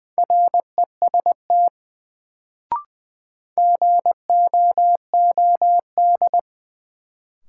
Morse code, heard as a keyed tone, about 20 wpm, 700 Hz. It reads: REST GOOD
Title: Sets of 2 Words - New Word Drill - 20wpm